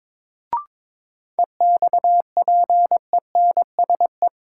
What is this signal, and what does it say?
Morse code, heard as a keyed tone, about 22 wpm, 700 Hz. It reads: EXPENSE
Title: Sets of 4 Words - New Word Drill - 22wpm